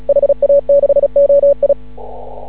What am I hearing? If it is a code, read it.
Morse code, heard as a keyed tone, about 33 wpm, 560 Hz. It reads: HA6OI
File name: CW / ha6oi 2